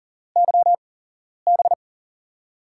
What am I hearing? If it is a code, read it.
Morse code, heard as a keyed tone, about 41 wpm, spaced as for 11 wpm, 700 Hz. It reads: YB
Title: USA Call Signs - 6 Characters - 8x Character Spacing - 40wpm